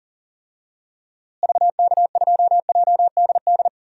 Morse code, heard as a keyed tone, about 40 wpm, 700 Hz. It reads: VK2JDD
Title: DXCC-100 Call Signs - 6 Characters - Repeated 2x - 40wpm